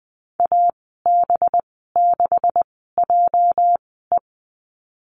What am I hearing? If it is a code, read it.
Morse code, heard as a keyed tone, about 20 wpm, spaced as for 14 wpm, 700 Hz. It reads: AB6JE
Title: 207 Lesson 33g - Letter j - Call Signs 20wpm